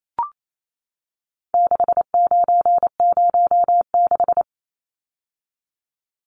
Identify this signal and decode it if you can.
Morse code, heard as a keyed tone, about 28 wpm, 700 Hz. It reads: 6906
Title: Numbers - 4-Digit 28wpm